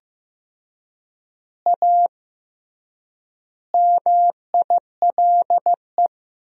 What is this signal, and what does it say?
Morse code, heard as a keyed tone, about 15 wpm, 700 Hz. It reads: A MILE
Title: 3x Word-Spacing - Sentences from Top 500 Words - 15wpm